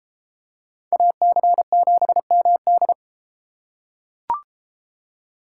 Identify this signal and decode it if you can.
Morse code, heard as a keyed tone, about 33 wpm, 700 Hz. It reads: AC7MD